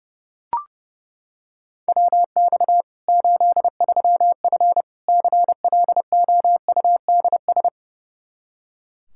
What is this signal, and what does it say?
Morse code, heard as a keyed tone, about 30 wpm, 700 Hz. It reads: WX 83F CLOUDS
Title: Practice QSO - Exchange 2 - Info - 30wpm